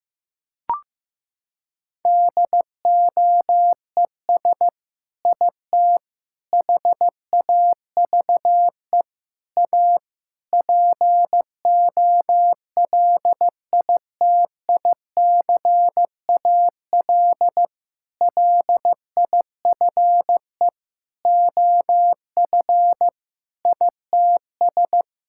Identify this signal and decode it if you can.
Morse code, heard as a keyed tone, about 15 wpm, 700 Hz. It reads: DOES IT HAVE A POLITICAL LIFE OF ITS
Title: Sentences from Top 1000 Words - Review Part 5 - 15wpm